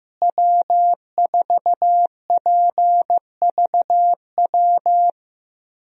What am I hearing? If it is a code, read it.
Morse code, heard as a keyed tone, about 15 wpm, 700 Hz. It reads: W4PVW
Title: USA Call Signs - 5 Characters - Repeated 3x - 15wpm